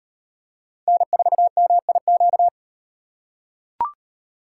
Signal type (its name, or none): Morse code